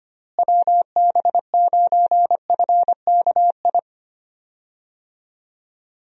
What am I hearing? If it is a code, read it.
Morse code, heard as a keyed tone, about 25 wpm, 700 Hz. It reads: WB9FKI
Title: USA Call Signs - 6 Characters - Repeated 3x - 25wpm